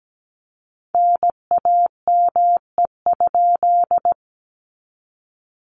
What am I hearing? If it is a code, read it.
Morse code, heard as a keyed tone, about 17 wpm, 700 Hz. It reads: NAME?
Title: Common QSO Phrases Encore 17wpm